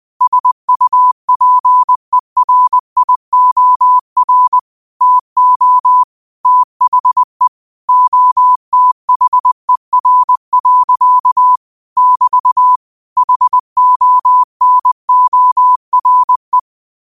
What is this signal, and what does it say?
Morse code, heard as a keyed tone, about 20 wpm, 1000 Hz. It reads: SUPERIOR TO THE OTHER. <BT> HONORE
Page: New quotes every day in morse code at 20 Words per minute.